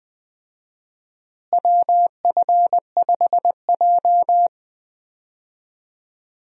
Morse code, heard as a keyed tone, about 20 wpm, 700 Hz. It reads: WF5J
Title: USA Call Signs - 4 Characters - Repeated 2x - 20wpm